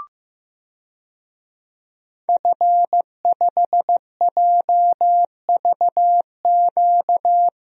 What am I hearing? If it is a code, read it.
Morse code, heard as a keyed tone, about 15 wpm, 700 Hz. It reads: F5JVQ
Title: DXCC-100 Call Signs - 5 Characters - Repeated 2x - 15wpm